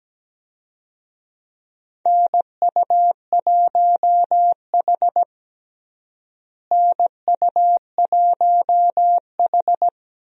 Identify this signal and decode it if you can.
Morse code, heard as a keyed tone, about 17 wpm, 700 Hz. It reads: NU1H NU1H
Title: USA Call Signs - 4 Characters - Repeated 3x - 17wpm